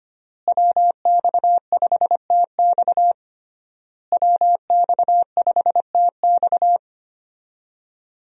Morse code, heard as a keyed tone, about 25 wpm, 700 Hz. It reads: WX5TX WX5TX
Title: USA Call Signs - 5 Characters - Repeated 3x - 25wpm